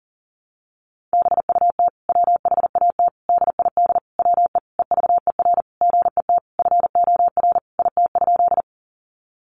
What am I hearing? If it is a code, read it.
Morse code, heard as a keyed tone, about 40 wpm, 700 Hz. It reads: BUT WHAT DID WE EVER GET FOR IT?
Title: Sentences from Top 400 Words 40wpm